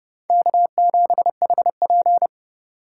Morse code, heard as a keyed tone, about 32 wpm, 700 Hz. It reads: K7HP
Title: USA Call Signs - 4 Characters - 3x Character Spacing - 30wpm